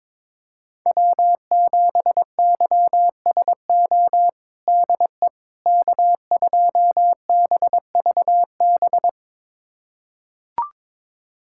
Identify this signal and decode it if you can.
Morse code, heard as a keyed tone, about 22 wpm, 700 Hz. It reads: W7YSO DE K2BVB
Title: USA Callsign DE Callsign - 5 Characters - 22wpm